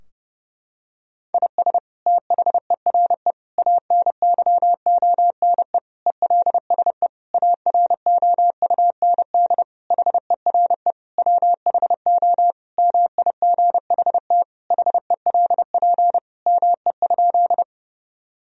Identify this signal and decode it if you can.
Morse code, heard as a keyed tone, about 30 wpm, 700 Hz. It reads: IS THERE ANYONE ELSE AROUND HERE WHO MIGHT HELP ME?
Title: Sentences from Top 1000 Words - Review Part 3 - 30wpm